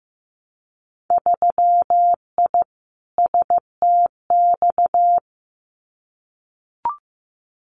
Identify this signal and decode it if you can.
Morse code, heard as a keyed tone, about 15 wpm, 700 Hz. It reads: 3I STX
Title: Winter Field Day Exchanges 15wpm